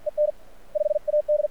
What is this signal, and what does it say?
Morse code, heard as a keyed tone, about 53 wpm, 600 Hz. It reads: EN 5 AD
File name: archived / CRSnet